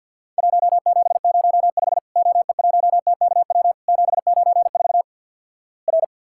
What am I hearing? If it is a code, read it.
Morse code, heard as a keyed tone, about 50 wpm, 700 Hz. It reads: 1705 OE1TKW 794 R
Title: Worked All Europe DX Contest - QTC Exchange - 50wpm